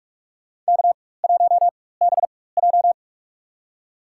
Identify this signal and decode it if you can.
Morse code, heard as a keyed tone, about 44 wpm, spaced as for 21 wpm, 700 Hz. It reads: K1BJ
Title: USA Call Signs - 4 Characters - 4x Character Spacing - 45wpm